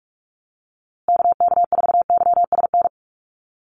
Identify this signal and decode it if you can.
Morse code, heard as a keyed tone, about 44 wpm, 700 Hz. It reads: KK4YSN
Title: USA Call Signs Repeated 2x 45wpm